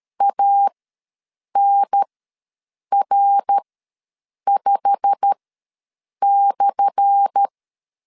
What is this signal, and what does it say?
Morse code, heard as a keyed tone, about 13 wpm, spaced as for 7 wpm, 800 Hz. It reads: ANR5/
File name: LEZIONI / 10 SIGNS / 10A SLASH ws 12wpm 3443